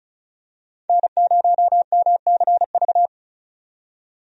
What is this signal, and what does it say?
Morse code, heard as a keyed tone, about 34 wpm, 700 Hz. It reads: N0MCV
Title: USA Call Signs - 5 Characters - 35wpm